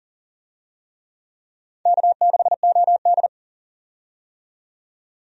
Morse code, heard as a keyed tone, about 37 wpm, 700 Hz. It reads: K6OD
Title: USA Call Signs - 4 Characters - 40wpm